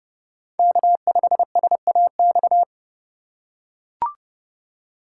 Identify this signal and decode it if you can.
Morse code, heard as a keyed tone, about 30 wpm, 700 Hz. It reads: K5SAX
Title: USA Call Signs - 5 Characters - 2x Character Spacing - 30wpm